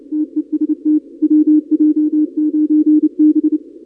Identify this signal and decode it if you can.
Morse code, heard as a keyed tone, about 28 wpm, 300 Hz. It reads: TEST WJ9B
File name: contests / 2019-cw-wpx-kt5j-k5pi- -2330 0-31